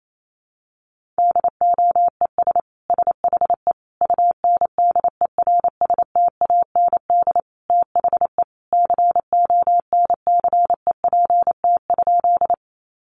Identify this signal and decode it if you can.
Morse code, heard as a keyed tone, about 28 wpm, 700 Hz. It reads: DOES SHE UNDERSTAND THE CONCEPT?
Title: Sentences from Top 1100 Words 28wpm